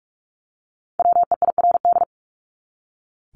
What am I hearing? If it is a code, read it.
Morse code, heard as a keyed tone, about 44 wpm, 700 Hz. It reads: WEIRD
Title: Top 1251-1500 Words 45wpm